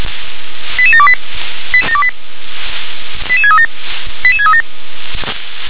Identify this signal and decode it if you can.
Five-tone selective call (ZVEI), heard as quick runs of five tones, tones of about 70 ms. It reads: 88728 88637 88637
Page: SDR/ZVEI